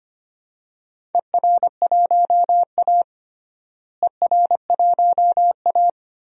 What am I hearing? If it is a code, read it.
Morse code, heard as a keyed tone, about 25 wpm, 700 Hz. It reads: ER1A ER1A
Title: DXCC-100 Call Signs - 4 Characters - Repeated 3x - 25wpm